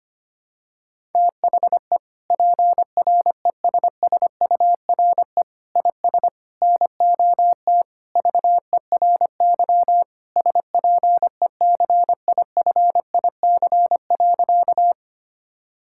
Morse code, heard as a keyed tone, about 25 wpm, 700 Hz. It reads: THE PRESSURE IS NOT VERY SPECIFIC.
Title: Sentences from Top 1000 Words - Review Part 6 - 25wpm